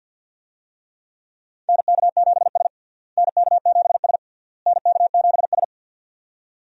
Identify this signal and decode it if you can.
Morse code, heard as a keyed tone, about 52 wpm, 700 Hz. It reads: NK7S NK7S NK7S
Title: USA Call Signs - 4 Characters - Repeated 3x - 50wpm